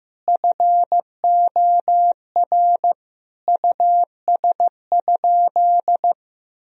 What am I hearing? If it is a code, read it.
Morse code, heard as a keyed tone, about 15 wpm, 700 Hz. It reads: FOR US?
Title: Sentences from Top 1000 Words - Review Part 3 - 15wpm